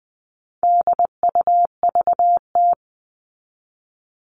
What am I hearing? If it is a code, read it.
Morse code, heard as a keyed tone, about 20 wpm, 700 Hz. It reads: DUVT
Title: All Cut Numbers - 4-Digit 20wpm